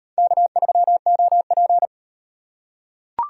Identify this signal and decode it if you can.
Morse code, heard as a keyed tone, about 37 wpm, 700 Hz. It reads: K3OP